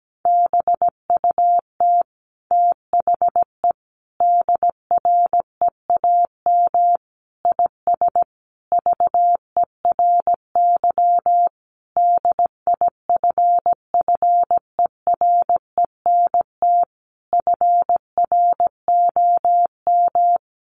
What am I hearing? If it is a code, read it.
Morse code, heard as a keyed tone, about 17 wpm, 700 Hz. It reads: BUT THE DREAM IS VERY DIFFERENT FROM
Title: Sentences from Top 1500 Words 17wpm